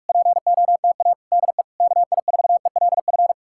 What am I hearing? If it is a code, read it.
Morse code, heard as a keyed tone, about 59 wpm, spaced as for 51 wpm, 700 Hz. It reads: POTA DE KI4ELF
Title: Speed-Racing Common QSO Phrases Encore - 45wpm